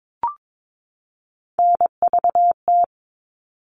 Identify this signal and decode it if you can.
Morse code, heard as a keyed tone, about 22 wpm, 700 Hz. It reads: NVT
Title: All Cut Numbers - 3-Digit 22wpm